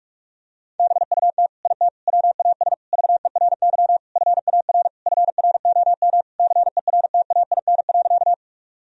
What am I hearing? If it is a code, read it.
Morse code, heard as a keyed tone, about 46 wpm, 700 Hz. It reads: BUT IT WAS VERY FAR FROM CERTAIN.
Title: Sentences from Top 1000 Words - Review Part 2 - 45wpm